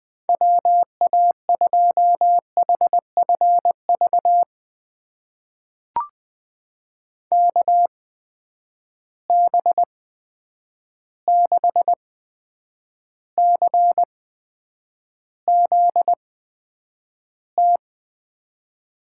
Morse code, heard as a keyed tone, about 20 wpm, spaced as for 9 wpm, 700 Hz. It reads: WA2HFV KB6CZT
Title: USA Call Signs - 6 Characters - 8x Character Spacing - 20wpm